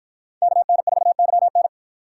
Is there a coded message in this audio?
Yes: Morse code from 0.4 to 1.7 s, heard as a keyed tone, about 53 wpm, 700 Hz. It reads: KN4YN